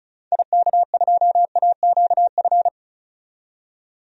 Morse code, heard as a keyed tone, about 36 wpm, 700 Hz. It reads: IK2AQF
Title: DXCC-100 Call Signs - 6 Characters - 35wpm